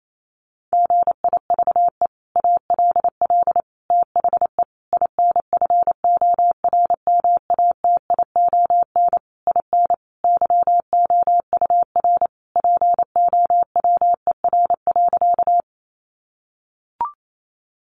Morse code, heard as a keyed tone, about 28 wpm, 700 Hz. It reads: GIVE ALL THE INFORMATION IN YOUR POWER.
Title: Sentences from Top 600 Words 28wpm